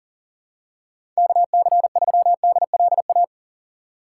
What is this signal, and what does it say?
Morse code, heard as a keyed tone, about 40 wpm, 700 Hz. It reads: KC3DLA
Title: USA Call Signs - 6 Characters - 3x Character Spacing - 40wpm